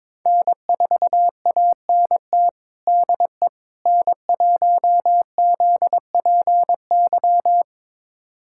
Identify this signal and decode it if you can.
Morse code, heard as a keyed tone, about 22 wpm, 700 Hz. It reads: N4ANT DE N1ZPY
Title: USA Callsign DE Callsign - 5 Characters - 22wpm